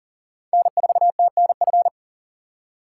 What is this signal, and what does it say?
Morse code, heard as a keyed tone, about 40 wpm, 700 Hz. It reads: N4TNF
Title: USA Call Signs - 4x Character Spacing - 40wpm